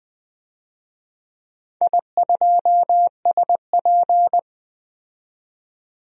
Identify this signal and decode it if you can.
Morse code, heard as a keyed tone, about 20 wpm, 700 Hz. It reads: I2SP
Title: DXCC-100 Call Signs - 4 Characters - 20wpm